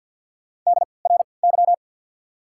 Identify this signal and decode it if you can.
Morse code, heard as a keyed tone, about 54 wpm, spaced as for 29 wpm, 690 Hz. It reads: DRY
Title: 3x Character Spacing - Top 500 Words - 50wpm